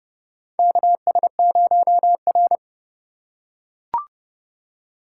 Morse code, heard as a keyed tone, about 30 wpm, 700 Hz. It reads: KS0R
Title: USA Call Signs - 4 Characters - 4x Character Spacing - 30wpm